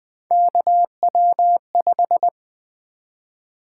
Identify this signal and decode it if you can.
Morse code, heard as a keyed tone, about 20 wpm, 700 Hz. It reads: KW5
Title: Three Letter-Number Combinations - 20wpm